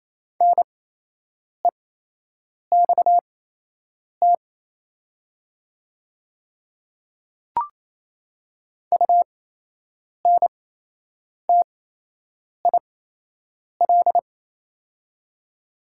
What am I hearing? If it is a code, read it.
Morse code, heard as a keyed tone, about 28 wpm, spaced as for 8 wpm, 700 Hz. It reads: NEXT UNTIL
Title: 8x Character Spacing - Top 300 Words - 28wpm